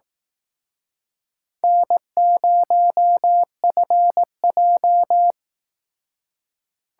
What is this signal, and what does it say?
Morse code, heard as a keyed tone, about 19 wpm, 700 Hz. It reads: N0FJ
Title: Speed Racing - USA Call Signs 15wpm